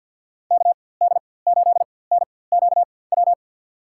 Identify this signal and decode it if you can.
Morse code, heard as a keyed tone, about 51 wpm, spaced as for 23 wpm, 700 Hz. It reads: KD8NQW
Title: USA Call Signs - 6 Characters - 4x Character Spacing - 50wpm